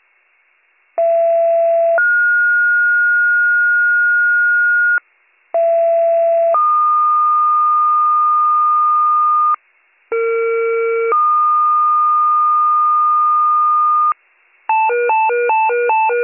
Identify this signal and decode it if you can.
Two-tone page, heard as a one-second tone, then a three-second tone, about 670 then 1430 Hz, about 670 then 1220 Hz, about 460 then 1220 Hz. It is heard from 1.0 to 14.1 s.